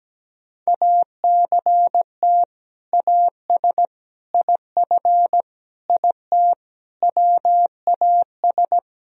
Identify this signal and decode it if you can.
Morse code, heard as a keyed tone, about 17 wpm, 700 Hz. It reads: ACT AS IF IT WAS
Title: Sentences from Top 1000 Words - Review Part 1 - 17wpm